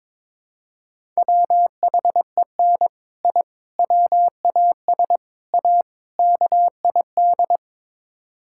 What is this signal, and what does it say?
Morse code, heard as a keyed tone, about 22 wpm, 700 Hz. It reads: WHEN I WAS A KID
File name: practice / 22wpm / Build A Phrase - 5-Word Phrases - 22wpm